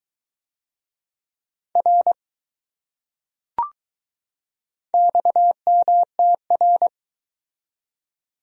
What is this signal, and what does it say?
Morse code, heard as a keyed tone, about 21 wpm, 700 Hz. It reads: R XMTR